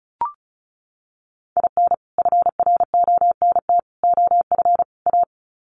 Morse code, heard as a keyed tone, about 34 wpm, 700 Hz. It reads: IN FRONT OF A
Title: Sets of 4 Words 35wpm